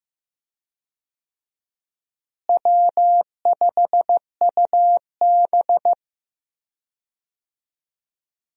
Morse code, heard as a keyed tone, about 15 wpm, 700 Hz. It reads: W5UB
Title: USA Call Signs - 4 Characters - Repeated 2x - 15wpm